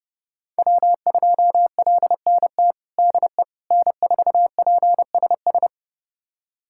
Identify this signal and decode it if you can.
Morse code, heard as a keyed tone, about 30 wpm, 700 Hz. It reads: W2LNT DE N4PSS
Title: USA Callsign DE Callsign - 5 Characters - 30wpm